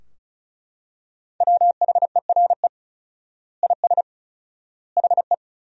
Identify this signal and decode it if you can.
Morse code, heard as a keyed tone, about 34 wpm, 700 Hz. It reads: WHERE IS HE
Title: 4x Word-Spacing - Sentences from Top 200 Words - 35wpm